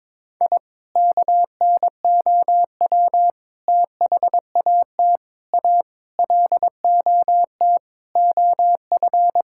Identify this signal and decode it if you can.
Morse code, heard as a keyed tone, about 22 wpm, 700 Hz. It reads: I KNOW THAT A LOT OF
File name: practice / 22wpm / Sentences with 4-Word Phrases - 22wpm